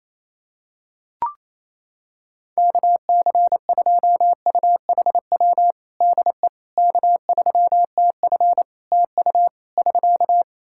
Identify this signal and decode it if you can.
Morse code, heard as a keyed tone, about 28 wpm, 700 Hz. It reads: KC2UHW DE K3TF TU <SK>
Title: Practice QSO - Exchange 4 - Ending - 28wpm